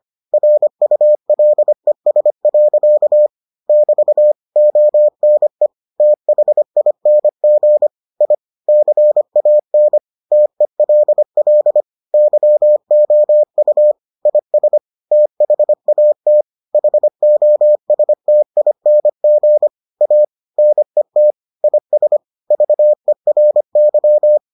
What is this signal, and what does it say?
Morse code, heard as a keyed tone, about 25 wpm, 600 Hz. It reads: RULES. <BT> ONE THING I CAN TELL YOU IS THAT HOSTING A NET IS VERY